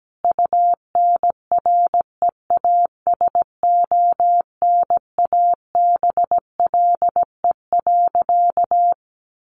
Morse code, heard as a keyed tone, about 17 wpm, 700 Hz. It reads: UNREASONABLE.
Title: A Sound Of Thunder 17wpm